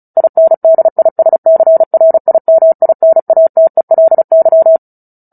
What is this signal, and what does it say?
Morse code, heard as a keyed tone, about 34 wpm, 650 Hz. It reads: INDISCRIMINATELY
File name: Book Two / 09-Chapter 9 - 35 WPM